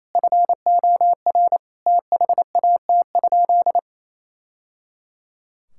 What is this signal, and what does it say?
Morse code, heard as a keyed tone, about 28 wpm, 700 Hz. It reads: FOR THAT?
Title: Sentences from Top 1000 Words - Review Part 5 - 28wpm